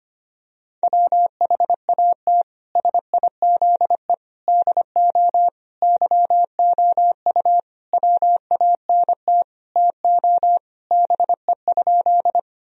Morse code, heard as a keyed tone, about 25 wpm, 700 Hz. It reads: WHAT SIZE DO YOU WANT TO BE?
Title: Sentences from Top 1000 Words - Review Part 2 - 25wpm